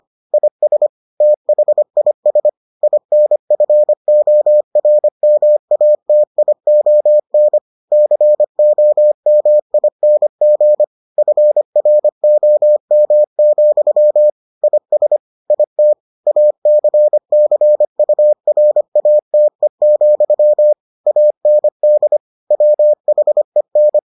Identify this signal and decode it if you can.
Morse code, heard as a keyed tone, about 25 wpm, 600 Hz. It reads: IS THIS INFORMATION COMING FROM, IS IT ACCURATE, AND WHEN